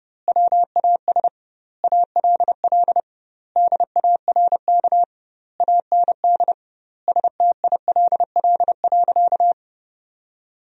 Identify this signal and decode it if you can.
Morse code, heard as a keyed tone, about 32 wpm, 700 Hz. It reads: WAS ALL DARK AND STILL.
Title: 2x Word-Spacing - Sentences from Top 500 Words - 30wpm